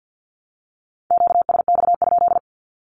Morse code, heard as a keyed tone, about 53 wpm, 700 Hz. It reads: QSX?
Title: Q-Codes 50wpm